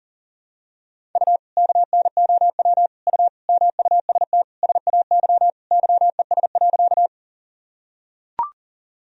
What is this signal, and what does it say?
Morse code, heard as a keyed tone, about 40 wpm, 700 Hz. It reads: U KNOW U MUST SAY YES.
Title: Sentences from Top 500 Words and Abbreviations 40wpm